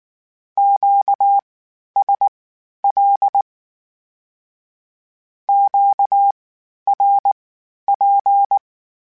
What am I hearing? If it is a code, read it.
Morse code, heard as a keyed tone, about 19 wpm, spaced as for 11 wpm, 800 Hz. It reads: QSL QRP
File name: (spaced) / Q-Calls Spaced-mp3s / Q Calls Twice sp @18